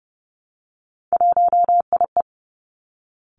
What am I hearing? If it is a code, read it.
Morse code, heard as a keyed tone, about 30 wpm, 700 Hz. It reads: 1IE
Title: Three Challenging Letter-Number Combinations 30wpm